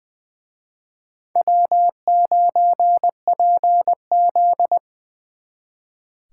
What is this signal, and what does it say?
Morse code, heard as a keyed tone, about 24 wpm, 700 Hz. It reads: W9PZ
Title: Speed Racing - USA 1x2 Call Signs 20wpm